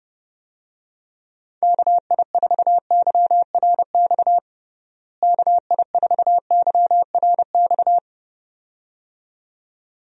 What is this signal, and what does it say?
Morse code, heard as a keyed tone, about 32 wpm, 700 Hz. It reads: KI4YRX KI4YRX
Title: USA Call Signs - 6 Characters - Repeated 2x - 30wpm